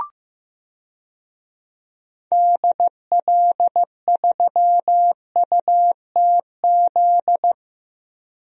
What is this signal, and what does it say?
Morse code, heard as a keyed tone, about 16 wpm, 700 Hz. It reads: DL3UTZ
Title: DXCC-100 Call Signs - 6 Characters - Repeated 3x - 15wpm